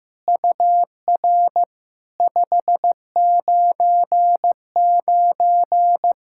Morse code, heard as a keyed tone, about 15 wpm, 700 Hz. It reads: UR 599
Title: SOTA QSO Practice 15wpm